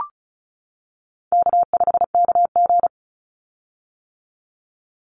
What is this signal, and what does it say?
Morse code, heard as a keyed tone, about 36 wpm, 700 Hz. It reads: K5KG
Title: Call Signs - 35wpm